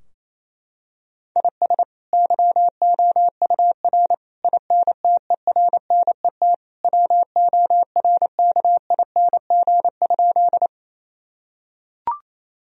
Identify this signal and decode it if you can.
Morse code, heard as a keyed tone, about 28 wpm, 700 Hz. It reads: IS YOUR INTERNET WORKING?
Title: Sentences from Top 1000 Words - Review Part 4 - 28wpm